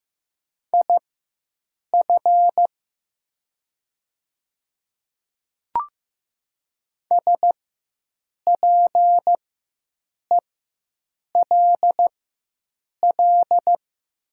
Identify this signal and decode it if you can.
Morse code, heard as a keyed tone, about 15 wpm, spaced as for 7 wpm, 700 Hz. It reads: IF SPELL
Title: 4x Character Spacing - Top 200 Words - 15wpm